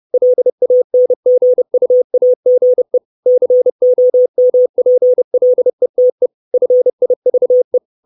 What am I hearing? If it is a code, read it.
Morse code, heard as a keyed tone, about 30 wpm, 500 Hz. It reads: LANGUAGE COMPLETE FIVE
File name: 632 Words 30WPM